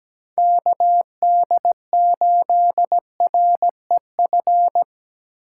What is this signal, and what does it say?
Morse code, heard as a keyed tone, about 20 wpm, 700 Hz. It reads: KD8REF
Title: Speed Racing - USA 2x3 Call Signs 17wpm